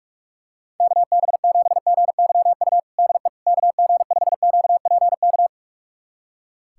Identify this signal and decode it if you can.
Morse code, heard as a keyed tone, about 46 wpm, 700 Hz. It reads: KB7GYU DE KG5QPK
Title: USA Callsign DE Callsign - 6 Characters - 45wpm